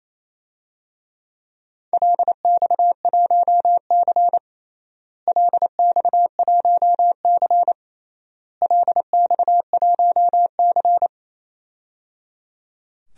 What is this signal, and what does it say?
Morse code, heard as a keyed tone, about 28 wpm, 700 Hz. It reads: LX1C LX1C LX1C
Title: DXCC-100 Call Signs - 4 Characters - Repeated 3x - 28wpm